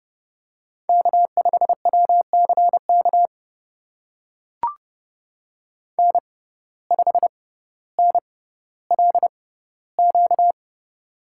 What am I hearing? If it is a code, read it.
Morse code, heard as a keyed tone, about 31 wpm, spaced as for 16 wpm, 700 Hz. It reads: K5WCK N5NLQ
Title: USA Call Signs - 5 Characters - 6x Character Spacing - 30wpm